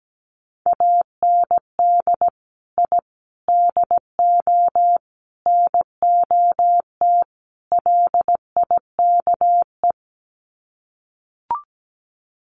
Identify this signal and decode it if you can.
Morse code, heard as a keyed tone, about 17 wpm, 700 Hz. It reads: AND I DO NOT LIKE
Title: Sets of 5 Words Encore2 17wpm